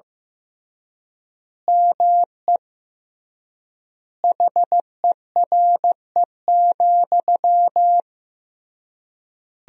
Morse code, heard as a keyed tone, about 15 wpm, 700 Hz. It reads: ME HERE,
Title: 3x Word-Spacing - Sentences from Top 500 Words - 15wpm